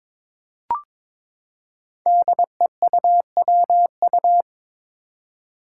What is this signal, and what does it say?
Morse code, heard as a keyed tone, about 22 wpm, 700 Hz. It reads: DEUWU
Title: Five Challenging Letter-Number Combinations 22wpm